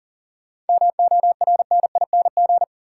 Morse code, heard as a keyed tone, about 40 wpm, 700 Hz. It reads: MORNING
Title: Sentences from Top 400 Words - New Word Drill - 40wpm